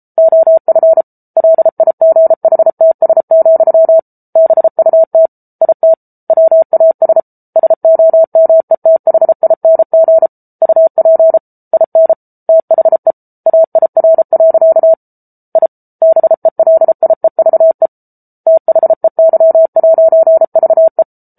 Morse code, heard as a keyed tone, about 34 wpm, 650 Hz. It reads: OF LIGHTS, BUT IT WAS SOMETHING UP IN THE AIR. I BELIEVE THEY'VE